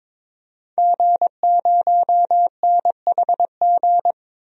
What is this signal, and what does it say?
Morse code, heard as a keyed tone, about 22 wpm, 700 Hz. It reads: G0NHG
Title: DXCC-100 Call Signs - 5 Characters - Repeated 2x - 22wpm